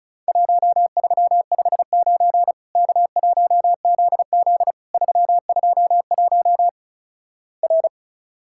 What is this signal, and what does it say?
Morse code, heard as a keyed tone, about 36 wpm, 700 Hz. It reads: 1359 K1ZZ 321 R
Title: Worked All Europe DX Contest - QTC Exchange - 35wpm